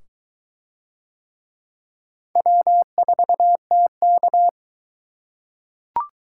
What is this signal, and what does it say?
Morse code, heard as a keyed tone, about 22 wpm, 700 Hz. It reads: W4TK